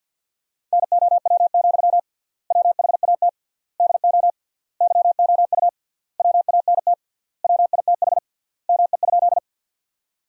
Morse code, heard as a keyed tone, about 48 wpm, 700 Hz. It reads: NOW, WHAT DO YOU WANT WITH ME?
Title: 3x Word-Spacing - Sentences from Top 200 Words - 50wpm